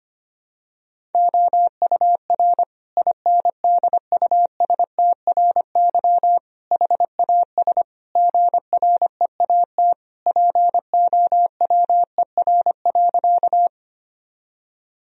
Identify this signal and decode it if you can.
Morse code, heard as a keyed tone, about 25 wpm, 700 Hz. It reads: OUR INDUSTRY HAS GREAT POWER.
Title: Sentences from Top 1000 Words - Review Part 4 - 25wpm